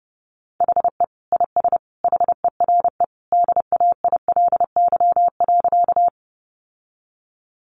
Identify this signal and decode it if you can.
Morse code, heard as a keyed tone, about 30 wpm, 700 Hz. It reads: HE IS HERE DAILY.
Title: Sentences from Top 1100 Words 30wpm